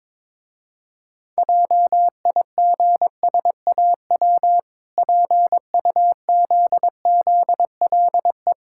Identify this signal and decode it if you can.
Morse code, heard as a keyed tone, about 23 wpm, 700 Hz. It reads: JIGSAW PUZZLE
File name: Easy Words - Speed-Racing with Hints - 22wpm